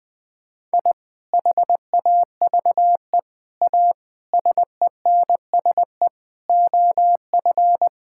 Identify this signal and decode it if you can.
Morse code, heard as a keyed tone, about 20 wpm, 700 Hz. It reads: I HAVE A SENSE OF
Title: Sentences with 3-Word Phrases - Spoken Beginning - 20wpm